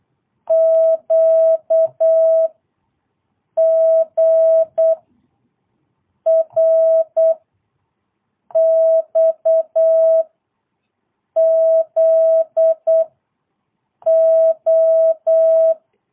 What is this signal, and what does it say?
Morse code, heard as a keyed tone, about 8 wpm, 650 Hz. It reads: Q G R X Z O
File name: jh8cbh / 2022cw